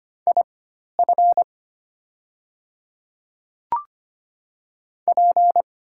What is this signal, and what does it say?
Morse code, heard as a keyed tone, about 25 wpm, spaced as for 12 wpm, 700 Hz. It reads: IF P
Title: 4x Character Spacing - Top 500 Words - 25wpm